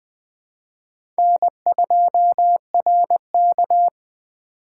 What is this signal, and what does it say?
Morse code, heard as a keyed tone, about 20 wpm, 700 Hz. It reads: N2RK
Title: USA Call Signs - 4 Characters - 3x Character Spacing - 20wpm